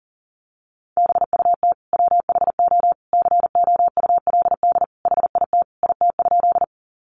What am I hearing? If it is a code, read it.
Morse code, heard as a keyed tone, about 40 wpm, 700 Hz. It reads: BUT WHO COULD HIT IT?
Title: Sentences from Top 1200 Words 40wpm